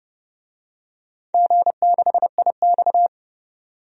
Morse code, heard as a keyed tone, about 30 wpm, 700 Hz. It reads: G6IX
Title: DXCC-100 Call Signs - 4 Characters - 30wpm